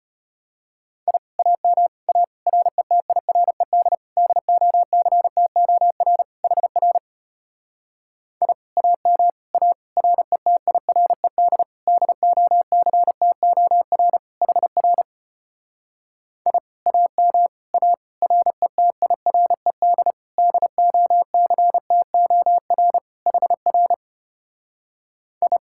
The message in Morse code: I AM A RETIRED DOCTOR HR I AM A RETIRED DOCTOR HR I AM A RETIRED DOCTOR HR I

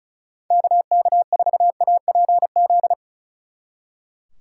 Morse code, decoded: KK4APZ

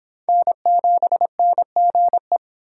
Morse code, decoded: N7NGE

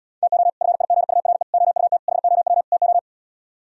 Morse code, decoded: TO FIGURE OUT HOW TO

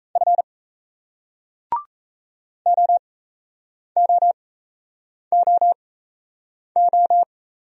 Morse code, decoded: F OOOO

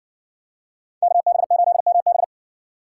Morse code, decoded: KB8MB

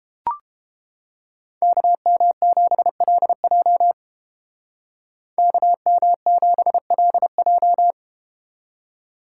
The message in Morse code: KM7LJ KM7LJ

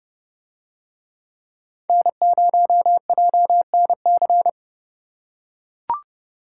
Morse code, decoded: N0JNC

